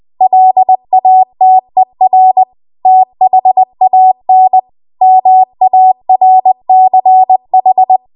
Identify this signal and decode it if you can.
Morse code, heard as a keyed tone, about 20 wpm, 750 Hz. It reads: LATER THAN MARCH
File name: Archive / 20 WPM / 20WPM